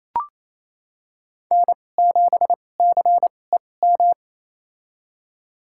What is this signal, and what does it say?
Morse code, heard as a keyed tone, about 28 wpm, spaced as for 20 wpm, 700 Hz. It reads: N7CEM
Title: USA Call Signs - 5 Characters - 2x Character Spacing - 28wpm